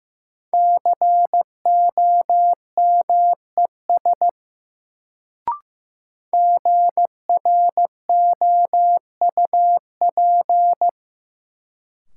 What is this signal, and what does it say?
Morse code, heard as a keyed tone, about 16 wpm, 700 Hz. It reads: COMES GROUP
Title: Sets of 3 Words - New Word Drill - 15wpm